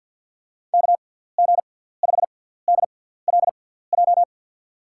Morse code, heard as a keyed tone, about 53 wpm, spaced as for 17 wpm, 700 Hz. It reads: KG5DLJ
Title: USA Call Signs - 6x Character Spacing - 50wpm